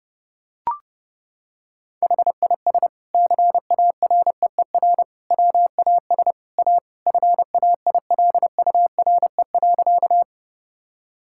Morse code, heard as a keyed tone, about 29 wpm, 700 Hz. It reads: HIS CAREER WAS A FAILURE.
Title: Sentences from Top 1000 Words - Review Part 6 - 30wpm